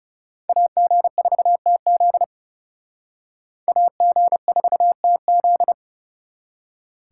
Morse code, heard as a keyed tone, about 37 wpm, 700 Hz. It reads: AG4TZ AG4TZ
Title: Speed Racing - USA 2x2 Call Signs 30wpm